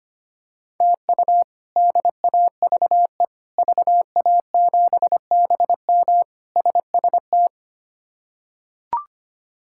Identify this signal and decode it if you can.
Morse code, heard as a keyed tone, about 25 wpm, 700 Hz. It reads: TU DAVE VA7BM SST
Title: K1USN SST QSO Practice - 25wpm